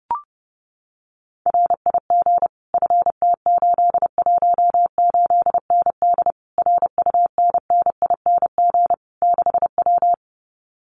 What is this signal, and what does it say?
Morse code, heard as a keyed tone, about 30 wpm, 700 Hz. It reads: RIG FT818ND RUNNING 6W
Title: Common QSO Phrases Encore 30wpm